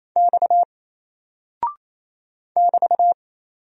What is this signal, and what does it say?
Morse code, heard as a keyed tone, about 27 wpm, 700 Hz. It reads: X <BT>
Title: Single Character (Letter:Number:Punctuation:Prosign) 28wpm